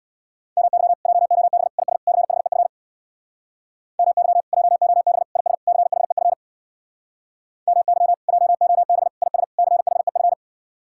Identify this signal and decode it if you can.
Morse code, heard as a keyed tone, about 63 wpm, 700 Hz. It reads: MY JOB IS CHEF MY JOB IS CHEF MY JOB IS CHEF